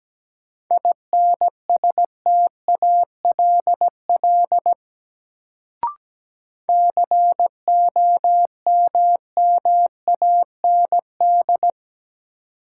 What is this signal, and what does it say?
Morse code, heard as a keyed tone, about 17 wpm, 700 Hz. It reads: INSTALL COMMAND
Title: Sentences from Top 1600 Words - New Word Drill - 17wpm